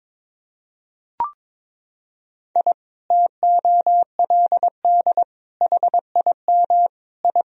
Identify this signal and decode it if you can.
Morse code, heard as a keyed tone, about 21 wpm, 700 Hz. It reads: I TOLD HIM I
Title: Sentences from Top 1000 Words - Review Part 1 - 22wpm